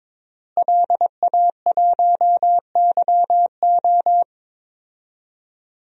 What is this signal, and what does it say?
Morse code, heard as a keyed tone, about 22 wpm, 700 Hz. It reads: LA1YO
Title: DXCC-100 Call Signs - 5 Characters - Repeated 3x - 22wpm